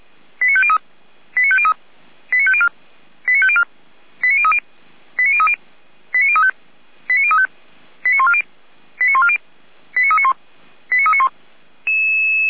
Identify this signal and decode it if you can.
Five-tone selective call (ZVEI), heard as quick runs of five tones, tones of about 70 ms. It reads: 89503 89503 89504 89504 89030 89030 89035 89035 89140 89140 89391 89391
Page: /455/SDR/ZVEI/